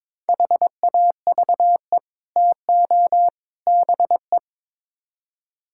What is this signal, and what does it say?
Morse code, heard as a keyed tone, about 22 wpm, 700 Hz. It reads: HAVE TO BE
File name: practice / 22wpm / Build A Phrase - 5-Word Phrases - 22wpm